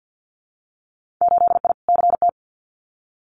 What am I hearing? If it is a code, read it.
Morse code, heard as a keyed tone, about 50 wpm, 700 Hz. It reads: 8I CT
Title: Winter Field Day Exchanges Encore 50wpm